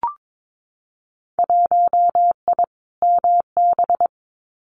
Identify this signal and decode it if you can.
Morse code, heard as a keyed tone, about 22 wpm, 700 Hz. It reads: 1I MB
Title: Winter Field Day Exchanges Encore 22wpm